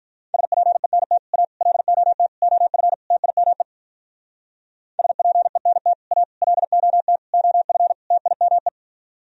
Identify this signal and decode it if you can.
Morse code, heard as a keyed tone, about 43 wpm, 700 Hz. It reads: SPENT A LOT OF TIME SPENT A LOT OF TIME
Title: Speed-Racing - Sets of 5 Words 35wpm